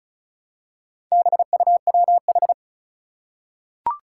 Morse code, heard as a keyed tone, about 36 wpm, 700 Hz. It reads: BUWH
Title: Four Challenging Letter-Number Combinations 35wpm